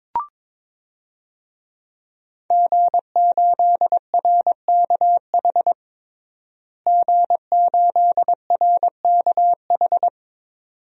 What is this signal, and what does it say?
Morse code, heard as a keyed tone, about 22 wpm, 700 Hz. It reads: G8RKH G8RKH
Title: DXCC-100 Call Signs - 5 Characters - Repeated 2x - 22wpm